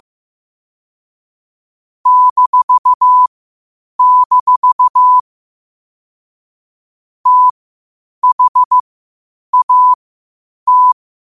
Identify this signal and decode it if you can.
Morse code, heard as a keyed tone, about 15 wpm, spaced as for 8 wpm, 1000 Hz. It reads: -- THAT
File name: qotdfiles / 8wpm